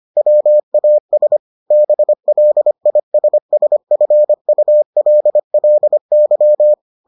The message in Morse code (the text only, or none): WAS BLISSFULLY